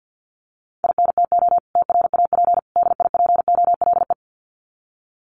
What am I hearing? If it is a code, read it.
Morse code, heard as a keyed tone, about 50 wpm, 700 Hz. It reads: INTO TRAP DIPOLE
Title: Common QSO Phrases 50wpm